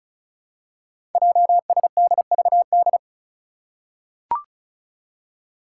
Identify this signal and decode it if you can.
Morse code, heard as a keyed tone, about 36 wpm, 700 Hz. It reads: JSDVD
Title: Code Groups - 5 Letters - 35wpm